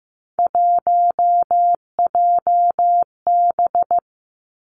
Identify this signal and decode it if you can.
Morse code, heard as a keyed tone, about 15 wpm, 700 Hz. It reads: 1JB
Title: Three Challenging Letter-Number Combinations 15wpm